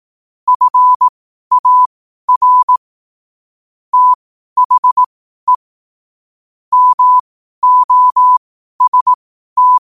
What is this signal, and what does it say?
Morse code, heard as a keyed tone, about 18 wpm, spaced as for 12 wpm, 1000 Hz. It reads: FAR THE MOST
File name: qotdfiles / 12wpm